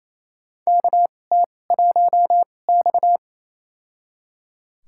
Morse code, heard as a keyed tone, about 28 wpm, spaced as for 20 wpm, 700 Hz. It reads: KT1X
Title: USA Call Signs - 4 Characters - 2x Character Spacing - 28wpm